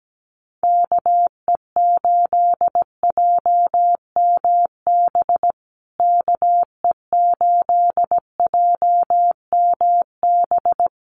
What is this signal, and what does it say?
Morse code, heard as a keyed tone, about 17 wpm, 700 Hz. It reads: KE8JMB KE8JMB
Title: USA Call Signs Repeated 3x 17wpm